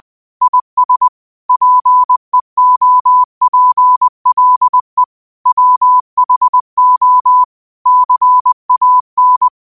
Morse code, heard as a keyed tone, about 20 wpm, 1000 Hz. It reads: IS PEOPLE WHO CAN